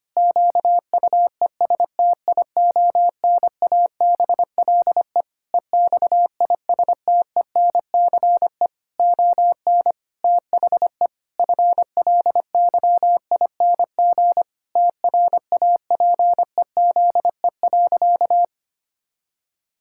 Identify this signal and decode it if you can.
Morse code, heard as a keyed tone, about 25 wpm, 700 Hz. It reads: QUESTIONABLE EXISTENCE ON THE FLYING TRAPEZE.
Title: Pangram Challenge v2 - 25wpm